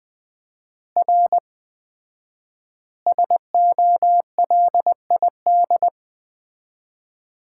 Morse code, heard as a keyed tone, about 20 wpm, 700 Hz. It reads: R SOLID
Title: Common QSO Phrases - 4x Word-Spacing - 20wpm